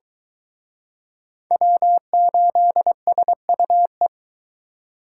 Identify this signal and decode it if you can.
Morse code, heard as a keyed tone, about 23 wpm, 700 Hz. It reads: W8SUE